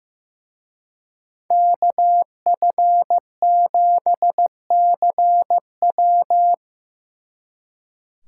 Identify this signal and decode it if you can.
Morse code, heard as a keyed tone, about 18 wpm, 700 Hz. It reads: KF7CW
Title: Speed Racing - USA 2x2 Call Signs 15wpm